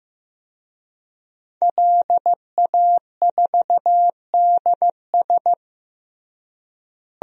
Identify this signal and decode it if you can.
Morse code, heard as a keyed tone, about 16 wpm, 700 Hz. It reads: LA4DS
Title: DXCC-100 Call Signs - 5 Characters - Repeated 3x - 15wpm